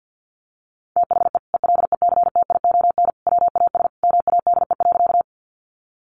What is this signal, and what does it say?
Morse code, heard as a keyed tone, about 50 wpm, 700 Hz. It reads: THE ELECTION WAS MADE.
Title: Sentences from Top 1500 Words 50wpm